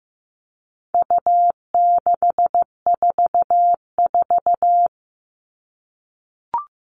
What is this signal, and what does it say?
Morse code, heard as a keyed tone, about 15 wpm, 700 Hz. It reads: U644
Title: All Cut Numbers - 4-Digit 15wpm